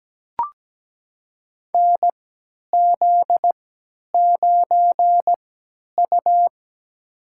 Morse code, heard as a keyed tone, about 17 wpm, spaced as for 10 wpm, 700 Hz. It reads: NZ9U
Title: USA Call Signs - 4 Characters - 3x Character Spacing - 17wpm